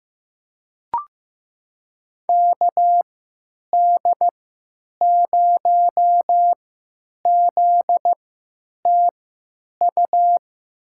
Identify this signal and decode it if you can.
Morse code, heard as a keyed tone, about 16 wpm, spaced as for 9 wpm, 700 Hz. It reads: KD0ZTU
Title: USA Call Signs - 6 Characters - 3x Character Spacing - 15wpm